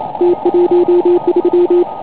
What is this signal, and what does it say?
Morse code, heard as a keyed tone, about 28 wpm, 350 Hz, with noise. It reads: T13
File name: Pile UP CUC204